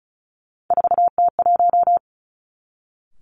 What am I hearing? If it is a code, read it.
Morse code, heard as a keyed tone, about 36 wpm, 700 Hz. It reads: 4T1
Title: Three Challenging Letter-Number Combinations 35wpm